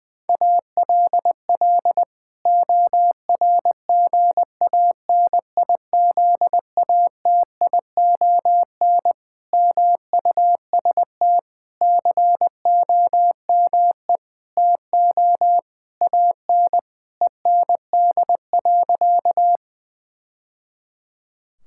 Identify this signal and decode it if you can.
Morse code, heard as a keyed tone, about 19 wpm, 700 Hz. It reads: ALL ORGANIZATION MUST COME TO AN END.
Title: Sentences from Top 1000 Words - Review Part 5 - 20wpm